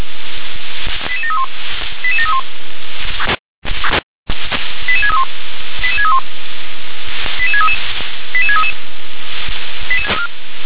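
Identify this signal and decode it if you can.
Five-tone selective call (ZVEI), heard as quick runs of five tones, tones of about 70 ms. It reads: 88731 88731 88631 88631 88633 88633
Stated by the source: ZVEI five-tone selective call received on SDR.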